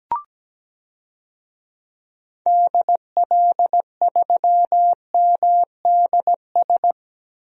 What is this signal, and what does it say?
Morse code, heard as a keyed tone, about 17 wpm, 700 Hz. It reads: DL3MDS
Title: DXCC-100 Call Signs - 6 Characters - 17wpm